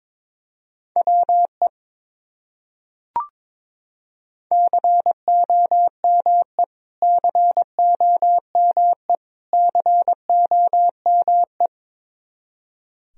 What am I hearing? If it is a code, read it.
Morse code, heard as a keyed tone, about 22 wpm, 700 Hz. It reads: WE COME COME COME
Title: Top 100 Words Repeated 22wpm